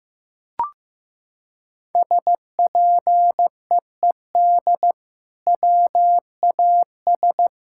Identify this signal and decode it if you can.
Morse code, heard as a keyed tone, about 15 wpm, 700 Hz. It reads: SPEED WAS
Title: Sentences from Top 1000 Words - Review Part 5 - 15wpm